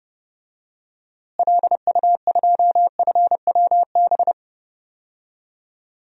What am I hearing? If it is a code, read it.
Morse code, heard as a keyed tone, about 30 wpm, 700 Hz. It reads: LU2FWB
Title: DXCC-100 Call Signs - 6 Characters - Repeated 2x - 30wpm